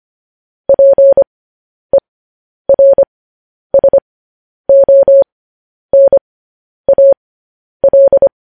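Morse code, heard as a keyed tone, about 25 wpm, spaced as for 10 wpm, 550 Hz. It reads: PERSONAL
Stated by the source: Morse code news headlines practice at 10 wpm.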